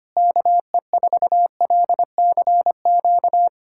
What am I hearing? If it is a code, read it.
Morse code, heard as a keyed tone, about 25 wpm, 700 Hz. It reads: KE4LCQ
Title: USA Callsign DE Callsign - 6 Characters - 25wpm